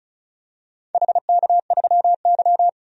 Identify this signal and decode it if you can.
Morse code, heard as a keyed tone, about 34 wpm, 700 Hz. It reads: HK3Y
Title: DXCC-100 Call Signs - 4 Characters - 35wpm